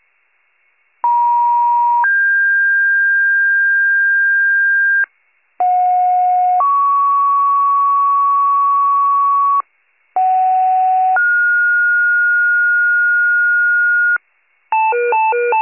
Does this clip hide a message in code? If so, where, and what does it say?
Yes: two-tone page from 1.0 to 14.2 s, heard as a one-second tone, then a three-second tone, about 950 then 1600 Hz, about 730 then 1120 Hz, about 750 then 1430 Hz, with noise.